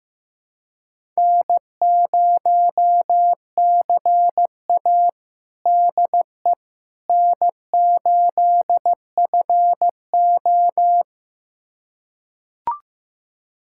Morse code, heard as a keyed tone, about 16 wpm, 700 Hz. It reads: N0CA DE N8FO
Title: USA Callsign DE Callsign - 4 Characters - 15wpm